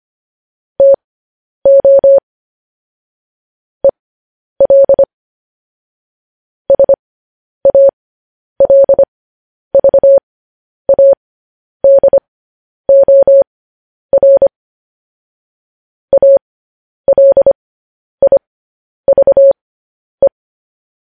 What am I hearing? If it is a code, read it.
Morse code, heard as a keyed tone, about 25 wpm, spaced as for 10 wpm, 550 Hz. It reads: TO EL SALVADOR ALIVE
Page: News Headlines in Morse Code at 10 WPM